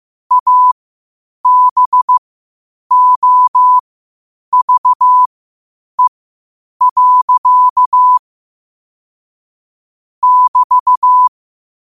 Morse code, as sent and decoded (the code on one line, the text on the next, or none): .- -... --- ...- . .-.-.- / -...-
ABOVE. <BT>